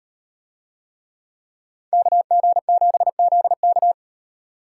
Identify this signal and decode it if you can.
Morse code, heard as a keyed tone, about 35 wpm, 700 Hz. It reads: KG7ZK